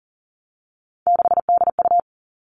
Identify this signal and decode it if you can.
Morse code, heard as a keyed tone, about 40 wpm, 700 Hz. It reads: 6DU
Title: Three Challenging Letter-Number Combinations 40wpm